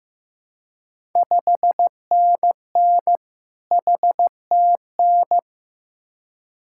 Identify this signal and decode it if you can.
Morse code, heard as a keyed tone, about 15 wpm, 690 Hz. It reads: 5NN HTN
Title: Ontario QSO Party Exchange 15wpm